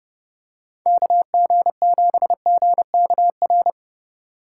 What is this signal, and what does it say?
Morse code, heard as a keyed tone, about 29 wpm, 700 Hz. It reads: KG7GKR